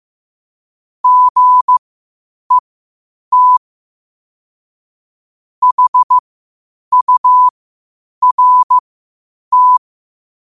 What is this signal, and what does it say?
Morse code, heard as a keyed tone, about 15 wpm, spaced as for 8 wpm, 1000 Hz. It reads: GET HURT
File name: qotdfiles / 8wpm